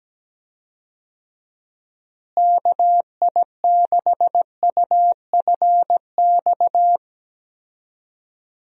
Morse code, heard as a keyed tone, about 17 wpm, 700 Hz. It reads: KI6UFX
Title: USA Call Signs - 6 Characters - Repeated 2x - 17wpm